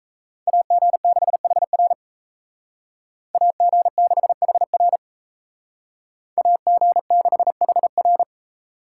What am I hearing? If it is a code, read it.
Morse code, heard as a keyed tone, about 33 wpm, 700 Hz. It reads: AG6HR AG6HR AG6HR
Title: Speed Racing - USA 2x2 Call Signs 28wpm